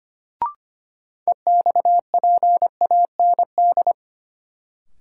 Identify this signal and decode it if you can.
Morse code, heard as a keyed tone, about 25 wpm, 700 Hz. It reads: EXPAND
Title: Sentences from Top 1400 Words - New Word Drill - 25wpm